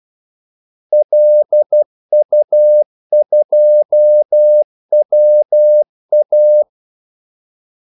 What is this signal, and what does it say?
Morse code, heard as a keyed tone, about 12 wpm, 600 Hz. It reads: LU2WA
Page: 2 ekstra ordmellemrum mellem kaldesignalerne